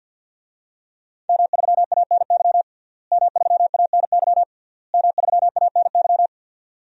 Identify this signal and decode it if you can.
Morse code, heard as a keyed tone, about 51 wpm, 700 Hz. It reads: M3ANY M3ANY M3ANY
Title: DXCC-100 Call Signs - 5 Characters - Repeated 3x - 50wpm